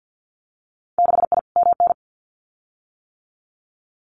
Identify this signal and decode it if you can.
Morse code, heard as a keyed tone, about 50 wpm, 700 Hz. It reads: 6I MN
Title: Winter Field Day Exchanges Encore 50wpm